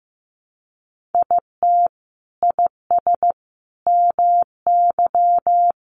Morse code, heard as a keyed tone, about 15 wpm, 700 Hz. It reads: IT IS MY
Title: Sentences from Top 1500 Words 15wpm